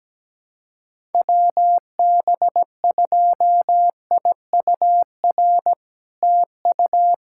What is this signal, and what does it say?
Morse code, heard as a keyed tone, about 17 wpm, 700 Hz. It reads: WB2IUR TU
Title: SOTA QSO Practice 17wpm